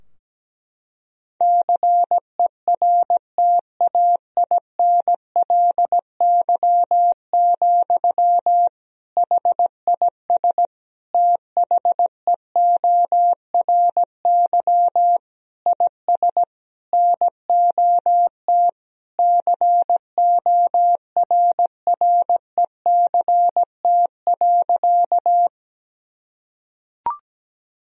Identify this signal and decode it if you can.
Morse code, heard as a keyed tone, about 17 wpm, 700 Hz. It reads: CERTAINLY, HIS THEORY IS NOT CORRECT.
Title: Sentences from Top 1000 Words - Review Part 3 - 17wpm